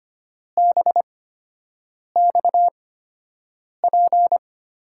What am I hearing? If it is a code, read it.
Morse code, heard as a keyed tone, about 25 wpm, spaced as for 7 wpm, 700 Hz. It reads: BXP
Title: USA Call Signs - 6 Characters - 8x Character Spacing - 25wpm